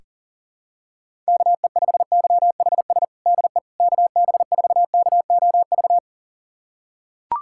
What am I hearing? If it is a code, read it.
Morse code, heard as a keyed tone, about 40 wpm, 700 Hz. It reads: KE5YHS DE KB4KOV
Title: USA Callsign DE Callsign - 6 Characters - 40wpm